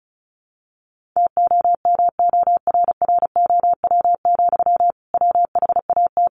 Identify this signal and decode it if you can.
Morse code, heard as a keyed tone, about 34 wpm, 700 Hz. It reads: TOMORROW, WHAT
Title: Sentences from Top 1500 Words 35wpm